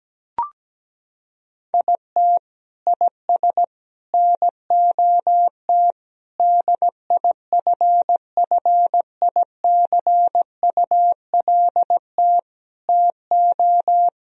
Morse code, heard as a keyed tone, about 17 wpm, 700 Hz. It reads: IT IS NOT DIFFICULT TO
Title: Sentences from Top 1000 Words - Review Part 4 - 17wpm